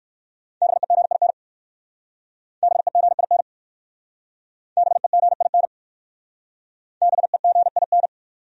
Morse code, heard as a keyed tone, about 60 wpm, 700 Hz. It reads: BEGIN BEGIN BEGIN BEGIN